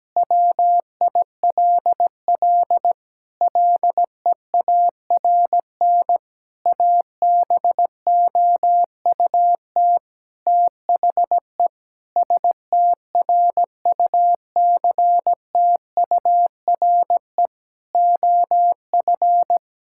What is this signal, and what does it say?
Morse code, heard as a keyed tone, about 17 wpm, 700 Hz. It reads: WILL LEARN ABOUT THE STRUCTURE OF
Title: Sentences from Top 1000 Words - Review Part 6 - 17wpm